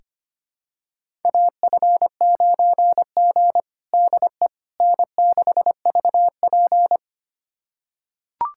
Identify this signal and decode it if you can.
Morse code, heard as a keyed tone, about 25 wpm, 700 Hz. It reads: AF9G DE N6VP
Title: USA Callsign DE Callsign - 4 Characters - 25wpm